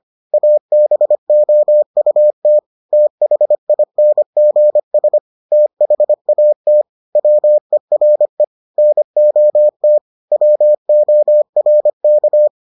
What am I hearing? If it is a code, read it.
Morse code, heard as a keyed tone, about 25 wpm, 600 Hz. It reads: ABOUT THINGS THAT WERE NOT WORK